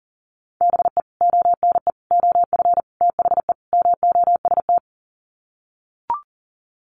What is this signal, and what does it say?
Morse code, heard as a keyed tone, about 40 wpm, 700 Hz. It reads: BE ONE OF THE MOST
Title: Sets of 5 Words Encore 40wpm